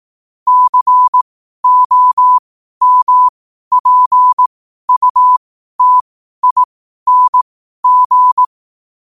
Morse code, heard as a keyed tone, about 18 wpm, spaced as for 12 wpm, 1000 Hz. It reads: COMPUTING
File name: qotdfiles / 12wpm